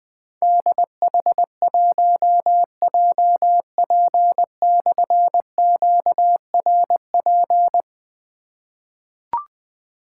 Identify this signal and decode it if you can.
Morse code, heard as a keyed tone, about 20 wpm, 700 Hz. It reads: DH1JP/QRP
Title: Difficult Call Signs 20wpm